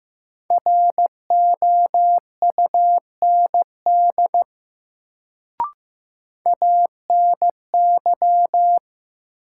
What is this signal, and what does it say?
Morse code, heard as a keyed tone, about 15 wpm, 700 Hz. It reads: ROUND ANY
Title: Sentences from Top 200 Words - New Word Drill - 15wpm